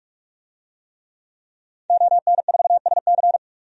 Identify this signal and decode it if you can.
Morse code, heard as a keyed tone, about 46 wpm, 700 Hz. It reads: ON4SC
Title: DXCC-100 Call Signs - 5 Characters - 45wpm